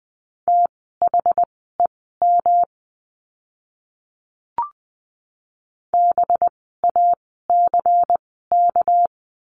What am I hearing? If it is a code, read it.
Morse code, heard as a keyed tone, about 20 wpm, spaced as for 15 wpm, 700 Hz. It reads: THEM BACK
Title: 206 Lesson 33f - Letter j - Words 20wpm